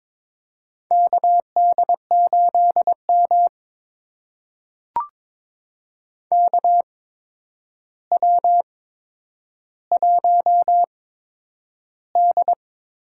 Morse code, heard as a keyed tone, about 22 wpm, spaced as for 9 wpm, 700 Hz. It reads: KD8M KW1D
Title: USA Call Signs - 4 Characters - 8x Character Spacing - 22wpm